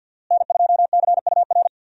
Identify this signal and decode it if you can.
Morse code, heard as a keyed tone, about 50 wpm, 700 Hz. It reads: N2CUR
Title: USA Call Signs - 5 Characters - Repeated 3x - 50wpm